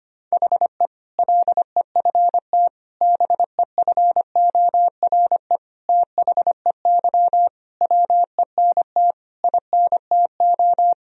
Morse code, heard as a keyed tone, about 25 wpm, 700 Hz. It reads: HE LEFT BEFORE THEY WENT INTO
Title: Sentences from Top 1000 Words - Review Part 4 - 25wpm